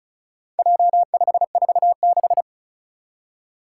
Morse code, heard as a keyed tone, about 36 wpm, 700 Hz. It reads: J546
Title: Four Challenging Letter-Number Combinations 35wpm